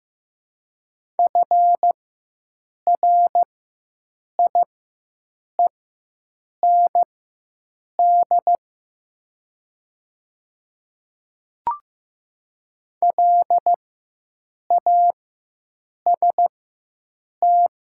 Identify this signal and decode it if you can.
Morse code, heard as a keyed tone, about 15 wpm, spaced as for 7 wpm, 700 Hz. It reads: FRIEND LAST
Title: 4x Character Spacing - Top 300 Words - 15wpm